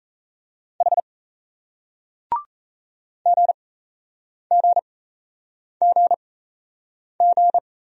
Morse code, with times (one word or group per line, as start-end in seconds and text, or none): 0.8-1.0 H
3.3-7.6 GGGG